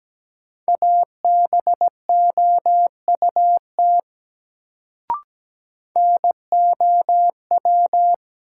Morse code, heard as a keyed tone, about 17 wpm, 700 Hz. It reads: ABOUT NOW
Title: Sentences from Top 100 Words - New Word Drill - 17wpm